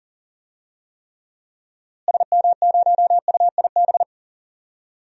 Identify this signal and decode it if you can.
Morse code, heard as a keyed tone, about 37 wpm, 700 Hz. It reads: SM0UIB
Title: DXCC-100 Call Signs - 6 Characters - Repeated 3x - 40wpm